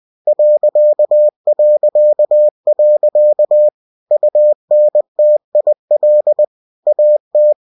Morse code, heard as a keyed tone, about 20 wpm, 600 Hz. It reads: ... UNTIL AT
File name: cw-books / winnie-the-pooh / winnie-20wpm-page-0015